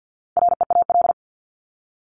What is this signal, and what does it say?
Morse code, heard as a keyed tone, about 50 wpm, 700 Hz. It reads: REAL
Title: Top 500 Words 50wpm